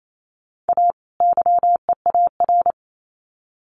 Morse code, heard as a keyed tone, about 28 wpm, 700 Hz. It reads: A YEAR
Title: Sets of 2 Words 28wpm